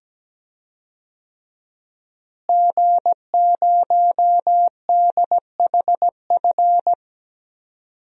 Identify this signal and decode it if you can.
Morse code, heard as a keyed tone, about 17 wpm, 700 Hz. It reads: G0DHF
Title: DXCC-100 Call Signs - 5 Characters - Repeated 2x - 17wpm